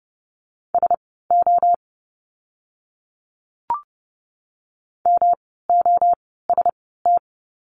Morse code, heard as a keyed tone, about 30 wpm, spaced as for 17 wpm, 700 Hz. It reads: SO MOST
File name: course / 30:10wpm / 230 Lesson 37f - Letter x - Words 30wpm